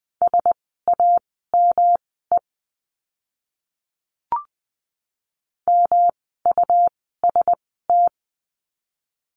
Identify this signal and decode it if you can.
Morse code, heard as a keyed tone, about 20 wpm, spaced as for 14 wpm, 700 Hz. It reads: SAME MUST
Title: 115 Lesson 19f - Letter w - Words 20wpm